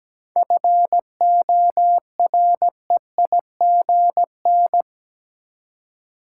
Morse code, heard as a keyed tone, about 17 wpm, 700 Hz. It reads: FOREIGN
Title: Seven Letter Words - 17wpm